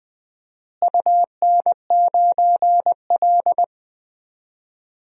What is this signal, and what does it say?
Morse code, heard as a keyed tone, about 20 wpm, 700 Hz. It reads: UN9L
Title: DXCC-100 Call Signs - 4 Characters - Repeated 3x - 20wpm